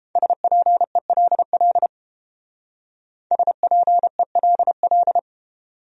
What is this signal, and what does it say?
Morse code, heard as a keyed tone, about 30 wpm, 700 Hz. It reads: SPELL SPELL